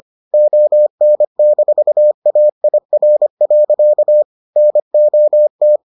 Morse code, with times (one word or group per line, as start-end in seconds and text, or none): 0.3-4.2 ON-AIR.
4.6-5.8 NOT